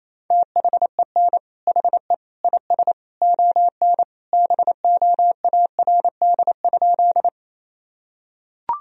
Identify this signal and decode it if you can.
Morse code, heard as a keyed tone, about 28 wpm, 700 Hz. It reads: THEN HE IS ON BOARD?
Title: Sentences from Top 1000 Words - Review Part 4 - 28wpm